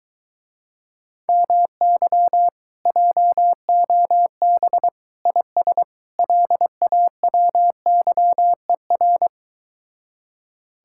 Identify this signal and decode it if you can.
Morse code, heard as a keyed tone, about 24 wpm, 700 Hz. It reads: MY JOB IS LAWYER